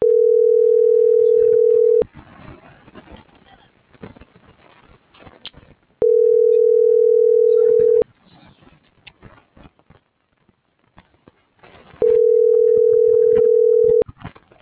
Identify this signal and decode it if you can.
Ringback tone (North America), 440 + 480 Hz.